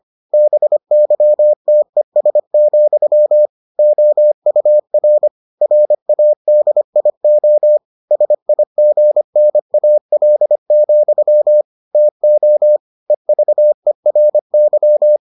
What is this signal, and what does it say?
Morse code, heard as a keyed tone, about 25 wpm, 600 Hz. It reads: BYTES, OUR RADIO SIGNAL, TO EVERY